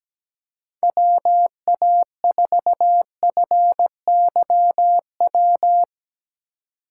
Morse code, heard as a keyed tone, about 17 wpm, 700 Hz. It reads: WA4FYW
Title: USA Call Signs - 6 Characters - 3x Character Spacing - 17wpm